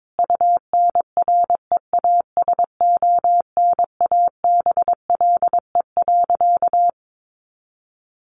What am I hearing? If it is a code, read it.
Morse code, heard as a keyed tone, about 22 wpm, 700 Hz. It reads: UNREASONABLE.
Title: A Sound Of Thunder 22wpm